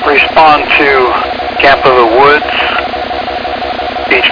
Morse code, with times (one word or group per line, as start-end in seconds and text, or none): none